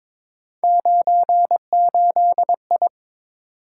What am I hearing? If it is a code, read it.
Morse code, heard as a keyed tone, about 23 wpm, 700 Hz. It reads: 98I
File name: practice / 22wpm / Three Letter-Number Combinations - 22wpm